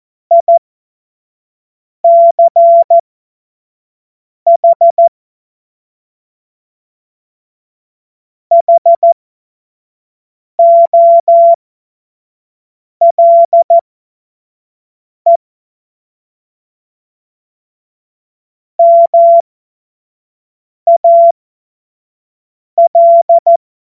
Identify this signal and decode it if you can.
Morse code, heard as a keyed tone, about 14 wpm, spaced as for 5 wpm, 680 Hz. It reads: ICH HOLE MAL
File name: Morsekurs / Witze / Witz 25. Oktober 2022 25 F70